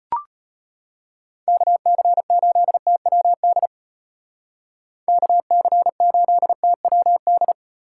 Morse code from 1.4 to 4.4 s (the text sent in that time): KC8TWD